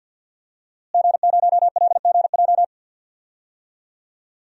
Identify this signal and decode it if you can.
Morse code, heard as a keyed tone, about 51 wpm, 700 Hz. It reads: G0LGJ
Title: DXCC-100 Call Signs - 5 Characters - Repeated 2x - 50wpm